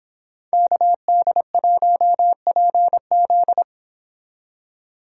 Morse code, heard as a keyed tone, about 26 wpm, 700 Hz. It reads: KD1PZ